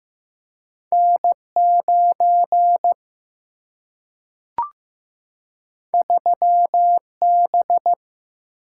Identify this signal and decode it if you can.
Morse code, heard as a keyed tone, about 15 wpm, 700 Hz. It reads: N9 3B
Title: Single-Letter Single-Number 15wpm